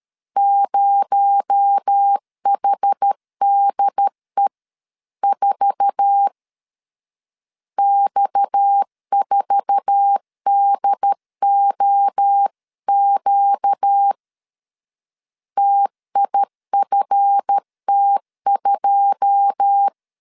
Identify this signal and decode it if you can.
Morse code, heard as a keyed tone, about 13 wpm, 800 Hz. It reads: 0HDE4 X4DOQ TIFT2
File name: CW / DET / 1 9 1500C S C 12wpm 3811